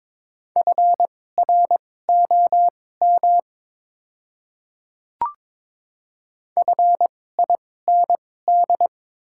Morse code, heard as a keyed tone, about 22 wpm, spaced as for 16 wpm, 700 Hz. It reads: FROM FIND
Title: 2x Character Spacing - Top 100 Words - 22wpm